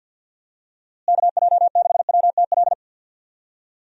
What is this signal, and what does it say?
Morse code, heard as a keyed tone, about 52 wpm, 700 Hz. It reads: KJ6WTL
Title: USA Call Signs - 6 Characters - Repeated 3x - 50wpm